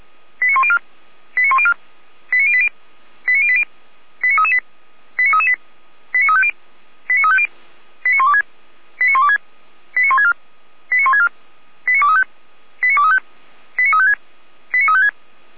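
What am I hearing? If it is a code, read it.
Five-tone selective call (ZVEI), heard as quick runs of five tones, tones of about 70 ms. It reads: 89104 89104 89080 89080 89338 89338 89350 89350 89126 89126 89164 89164 89235 89235 89356 89356
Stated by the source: Five-tone ZVEI selective call received on SDR.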